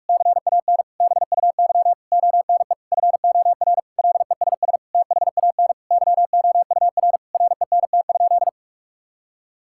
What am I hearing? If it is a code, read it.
Morse code, heard as a keyed tone, about 46 wpm, 690 Hz. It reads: CAN BUY ONE FOR LESS THAN YOUR RENT?
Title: Sentences from Top 1000 Words - Review Part 5 - 45wpm